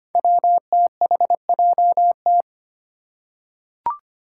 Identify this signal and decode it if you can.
Morse code, heard as a keyed tone, about 25 wpm, 700 Hz. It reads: WTHJT
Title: Five Challenging Letter-Number Combinations 25wpm